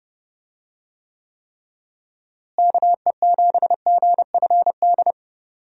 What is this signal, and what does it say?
Morse code, heard as a keyed tone, about 30 wpm, 700 Hz. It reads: KE7GFD
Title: USA Call Signs - 6 Characters - Repeated 3x - 30wpm